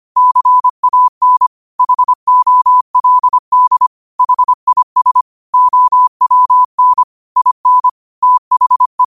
Morse code, heard as a keyed tone, about 25 wpm, 1000 Hz. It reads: CAN HOLD HIS OWN IN THE
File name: qotdfiles / 25wpm